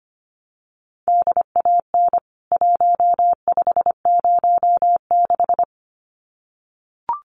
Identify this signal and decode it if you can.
Morse code, heard as a keyed tone, about 25 wpm, 700 Hz. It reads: DAN 1506
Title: CWT Exchange 25wpm